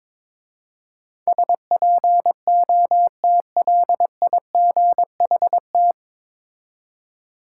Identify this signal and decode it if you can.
Morse code, heard as a keyed tone, about 22 wpm, 700 Hz. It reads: SPOTLIGHT
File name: 9 Letter Words - First 3 Letters Given - 22wpm